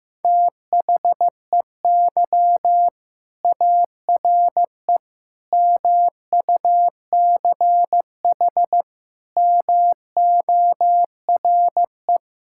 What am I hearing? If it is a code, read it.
Morse code, heard as a keyed tone, about 15 wpm, 700 Hz. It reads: THEY ARE MUCH MORE
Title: Sentences from Top 1000 Words - Review Part 5 - 15wpm